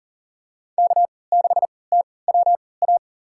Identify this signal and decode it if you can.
Morse code, heard as a keyed tone, about 40 wpm, spaced as for 23 wpm, 700 Hz. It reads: K6TWA
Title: USA Call Signs - 5 Characters - 3x Character Spacing - 40wpm